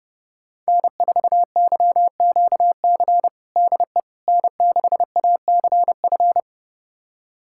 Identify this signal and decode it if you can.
Morse code, heard as a keyed tone, about 30 wpm, 700 Hz. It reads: N4YQC DE N6ACF
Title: USA Callsign DE Callsign - 5 Characters - 30wpm